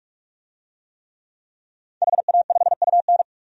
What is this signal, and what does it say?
Morse code, heard as a keyed tone, about 46 wpm, 700 Hz. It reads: HA5UN
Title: DXCC-100 Call Signs - 5 Characters - 45wpm